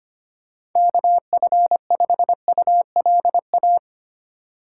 Morse code, heard as a keyed tone, about 25 wpm, 700 Hz. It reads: KF5ULA
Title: USA Call Signs - 6 Characters - 2x Character Spacing - 25wpm